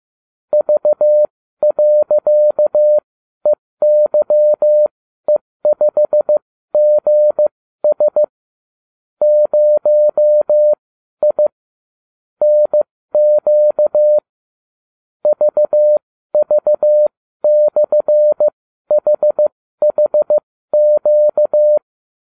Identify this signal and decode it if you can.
Morse code, heard as a keyed tone, about 15 wpm, spaced as for 12 wpm, 600 Hz. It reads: V.EYE5GS 0I NQ VV/HHQ